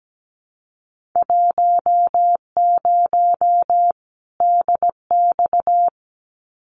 Morse code, heard as a keyed tone, about 17 wpm, 700 Hz. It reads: 10 DX
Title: Winter Field Day Exchanges 17wpm